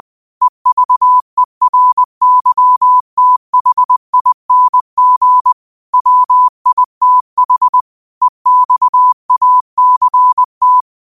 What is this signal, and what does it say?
Morse code, heard as a keyed tone, about 20 wpm, 1000 Hz. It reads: EVERYTHING WITH EXACT
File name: qotdfiles / 20wpm